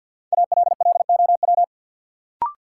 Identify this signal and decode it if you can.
Morse code, heard as a keyed tone, about 51 wpm, 700 Hz. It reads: ALLOW
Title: Sentences from Top 700 Words - New Word Drill - 50wpm